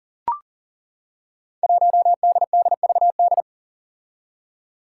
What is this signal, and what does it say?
Morse code, heard as a keyed tone, about 40 wpm, 700 Hz. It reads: 1DDVD
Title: Five Challenging Letter-Number Combinations 40wpm